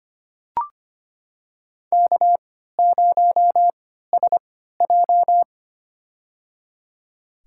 Morse code, heard as a keyed tone, about 25 wpm, spaced as for 14 wpm, 700 Hz. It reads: K0SJ
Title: USA Call Signs - 4 Characters - 3x Character Spacing - 25wpm